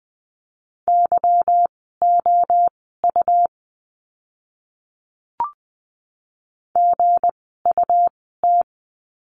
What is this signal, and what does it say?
Morse code, heard as a keyed tone, about 20 wpm, spaced as for 14 wpm, 700 Hz. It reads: YOU GUT
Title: 160 Lesson 26f - Letter g - Words 20wpm